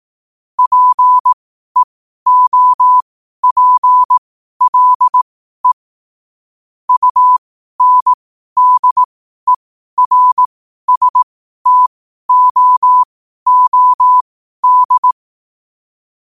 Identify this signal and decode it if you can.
Morse code, heard as a keyed tone, about 18 wpm, spaced as for 12 wpm, 1000 Hz. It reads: PEOPLE UNDERSTOOD
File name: qotdfiles / 12wpm